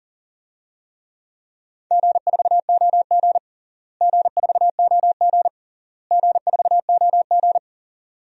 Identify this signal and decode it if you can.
Morse code, heard as a keyed tone, about 40 wpm, 700 Hz. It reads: G4OG G4OG G4OG
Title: DXCC-100 Call Signs - 4 Characters - Repeated 3x - 40wpm